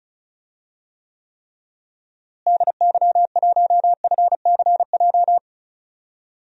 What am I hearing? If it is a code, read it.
Morse code, heard as a keyed tone, about 36 wpm, 700 Hz. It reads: DY1FCJ
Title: DXCC-100 Call Signs - 6 Characters - Repeated 3x - 35wpm